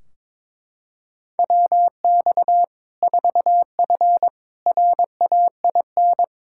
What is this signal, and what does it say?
Morse code, heard as a keyed tone, about 21 wpm, 700 Hz. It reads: WX 4F RAIN
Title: Practice QSO - Exchange 2 - Info - 22wpm